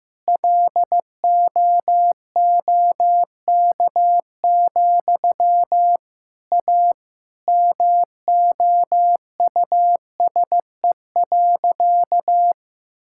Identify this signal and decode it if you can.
Morse code, heard as a keyed tone, about 15 wpm, 700 Hz. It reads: LOOK, A MOUSE.
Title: Sentences from Top 1000 Words - Review Part 5 - 15wpm